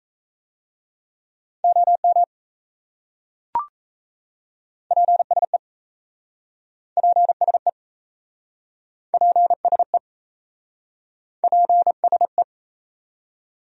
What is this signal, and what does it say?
Morse code, heard as a keyed tone, about 38 wpm, 700 Hz. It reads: OM PSE PSE PSE PSE